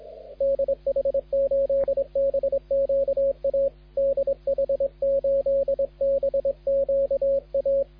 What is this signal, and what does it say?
Morse code, heard as a keyed tone, about 23 wpm, 250 Hz, with radio noise. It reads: DH8BQA DH8BQA
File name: HB9CZF de DH8BQA CQWWCW CW 21MHz